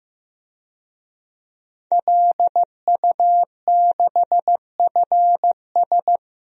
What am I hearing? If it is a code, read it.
Morse code, heard as a keyed tone, about 16 wpm, 700 Hz. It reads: LU6FS
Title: DXCC-100 Call Signs - 5 Characters - Repeated 2x - 15wpm